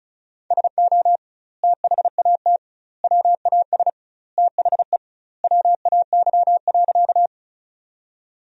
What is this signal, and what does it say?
Morse code, heard as a keyed tone, about 36 wpm, 700 Hz. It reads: SO THAT WAS THE WAY.
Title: 2x Word-Spacing - Sentences from Top 100 Words - 35wpm